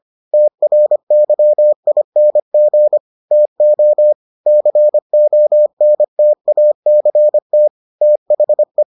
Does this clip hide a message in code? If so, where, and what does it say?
Yes: Morse code from 0.3 to 8.8 s, heard as a keyed tone, about 25 wpm, 600 Hz. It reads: TRYING TO CONTACT THE